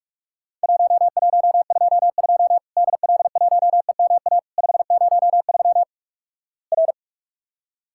Morse code, heard as a keyed tone, about 47 wpm, 700 Hz. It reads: 1122 DL1EMA 503 R
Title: Worked All Europe DX Contest - QTC Exchange - 45wpm